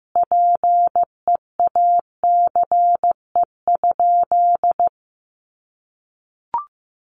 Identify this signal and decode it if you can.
Morse code, heard as a keyed tone, about 15 wpm, 700 Hz. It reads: PEACE?
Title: Sentences from Top 1500 Words 15wpm